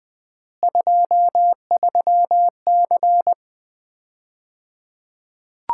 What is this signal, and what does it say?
Morse code, heard as a keyed tone, about 21 wpm, 700 Hz. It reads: 23C
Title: Common QSO Phrases - 3x Word-Spacing - 20wpm